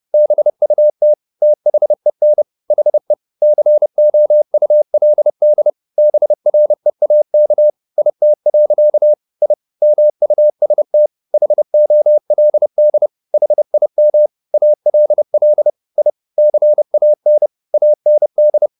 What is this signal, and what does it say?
Morse code, heard as a keyed tone, about 30 wpm, 600 Hz. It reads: BUT THEN HE COULD BREAK IT. I MUST HOLD HIM ALL I CAN AND
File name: the old man and the sea / the old man and the sea 1 30wpm